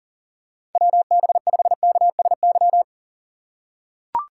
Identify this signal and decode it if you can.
Morse code, heard as a keyed tone, about 37 wpm, 700 Hz. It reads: WB5KSY
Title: USA Call Signs - 6 Characters - 8x Character Spacing - 40wpm